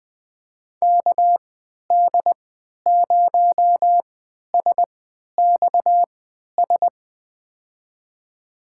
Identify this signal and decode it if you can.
Morse code, heard as a keyed tone, about 21 wpm, spaced as for 12 wpm, 700 Hz. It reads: KD0SXS
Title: USA Call Signs - 6 Characters - 3x Character Spacing - 20wpm